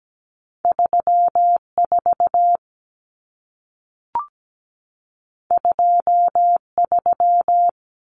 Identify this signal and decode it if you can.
Morse code, heard as a keyed tone, about 17 wpm, 700 Hz. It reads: 34 23
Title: Numbers - 2-Digit 17wpm